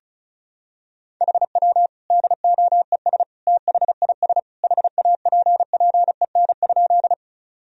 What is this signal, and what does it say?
Morse code, heard as a keyed tone, about 34 wpm, 700 Hz. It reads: HW DOES THIS HAPPEN?
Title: Sentences from Top 400 Words and Abbreviations 35wpm